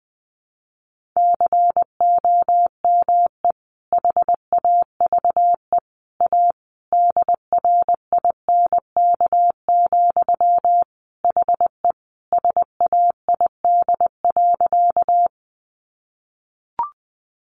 Morse code, heard as a keyed tone, about 20 wpm, 700 Hz. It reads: COME HAVE A DRINK, HE SAID.
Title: Sentences from Top 1500 Words 20wpm